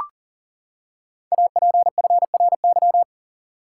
Morse code, heard as a keyed tone, about 42 wpm, 700 Hz. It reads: APFRY
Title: Code Groups - 5 Letters - 40wpm